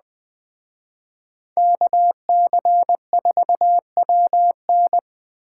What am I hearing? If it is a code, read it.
Morse code, heard as a keyed tone, about 20 wpm, 700 Hz. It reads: KC4WN
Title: Speed Racing - USA Call Signs 17wpm